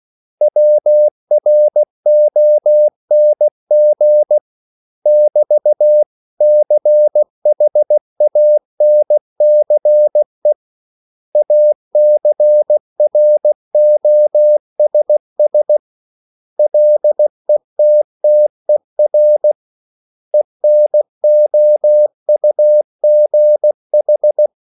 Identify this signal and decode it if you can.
Morse code, heard as a keyed tone, about 16 wpm, 600 Hz. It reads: WRONG <BT> CHANCE ACROSS LETTER ENOUGH
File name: CW / Practice / 500MostCommonEnglishWords16